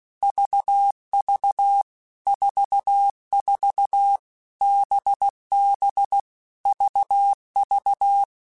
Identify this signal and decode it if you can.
Morse code, heard as a keyed tone, about 16 wpm, 800 Hz. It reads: VV 44 BB VV
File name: Basic Letter Groups By7Groups / VB46 / VB46@15